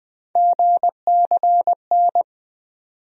Morse code, heard as a keyed tone, about 20 wpm, 700 Hz. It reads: GCN
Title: USA Call Sign Postfixes - 3 Letters 20wpm